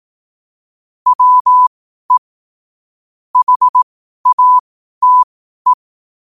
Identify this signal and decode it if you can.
Morse code, heard as a keyed tone, about 18 wpm, spaced as for 12 wpm, 1000 Hz. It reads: WE HATE
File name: qotdfiles / 12wpm